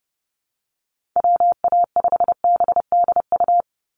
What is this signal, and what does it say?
Morse code, heard as a keyed tone, about 30 wpm, 700 Hz. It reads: WA5BDU
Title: Call Signs Repeated 2x - 30wpm